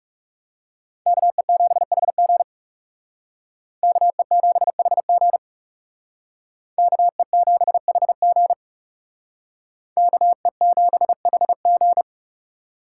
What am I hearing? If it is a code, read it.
Morse code, heard as a keyed tone, about 38 wpm, 700 Hz. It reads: KE7HG KE7HG KE7HG KE7HG